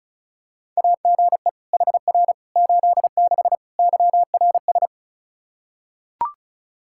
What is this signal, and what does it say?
Morse code, heard as a keyed tone, about 36 wpm, 700 Hz. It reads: AGE HR 86 YRS
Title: Practice QSO - Exchange 3 - More Info - 35wpm